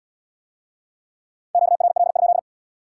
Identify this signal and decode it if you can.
Morse code, heard as a keyed tone, about 74 wpm, 700 Hz. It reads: QRL?